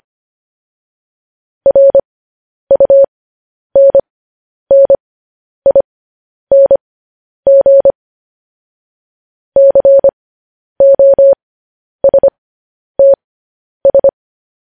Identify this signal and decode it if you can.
Morse code, heard as a keyed tone, about 25 wpm, spaced as for 10 wpm, 550 Hz. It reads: RUNNING COSTS
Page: news-headlines-in-morse-code-at-10-wpm